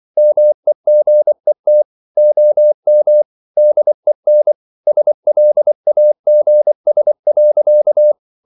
Morse code, heard as a keyed tone, about 24 wpm, 600 Hz. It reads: MEGET OM DEN SLAGS.
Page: Ferie 24 wpm